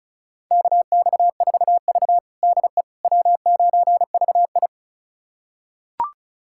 Morse code, heard as a keyed tone, about 34 wpm, 700 Hz. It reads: KX4V DE W9VI
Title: USA Callsign DE Callsign - 4 Characters - 35wpm